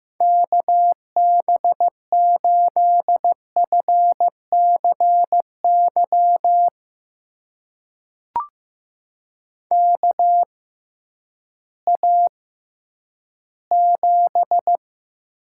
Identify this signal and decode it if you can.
Morse code, heard as a keyed tone, about 15 wpm, spaced as for 10 wpm, 700 Hz. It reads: KB8FCY KA7
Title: USA Call Signs - 6 Characters - 6x Character Spacing - 15wpm